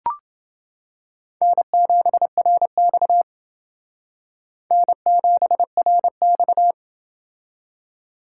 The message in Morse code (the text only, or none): N7RX N7RX